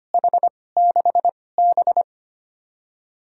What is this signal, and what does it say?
Morse code, heard as a keyed tone, about 25 wpm, 700 Hz. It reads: H 6 B
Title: USA Call Signs - 4 Characters - 2x Character Spacing - 25wpm